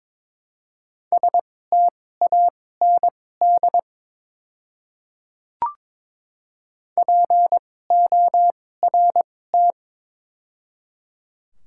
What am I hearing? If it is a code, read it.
Morse code, heard as a keyed tone, about 22 wpm, spaced as for 16 wpm, 700 Hz. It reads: STAND PORT
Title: 2x Character Spacing - Top 500 Words - 22wpm